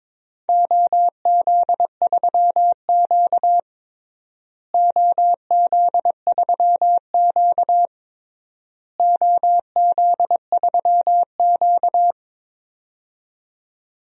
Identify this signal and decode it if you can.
Morse code, heard as a keyed tone, about 22 wpm, 700 Hz. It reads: OZ3Q OZ3Q OZ3Q
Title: DXCC-100 Call Signs - 4 Characters - Repeated 3x - 22wpm